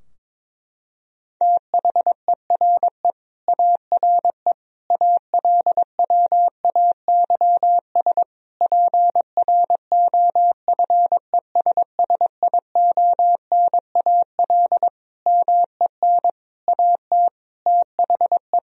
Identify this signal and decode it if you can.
Morse code, heard as a keyed tone, about 22 wpm, 700 Hz. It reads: THERE ARE ALWAYS PROFESSIONAL MEN AT THE
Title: Sentences from Top 1000 Words - Review Part 4 - 22wpm